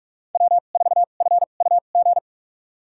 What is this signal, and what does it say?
Morse code, heard as a keyed tone, about 46 wpm, spaced as for 33 wpm, 700 Hz. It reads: W4FUG
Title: USA Call Signs - 5 Characters - 2x Character Spacing - 45wpm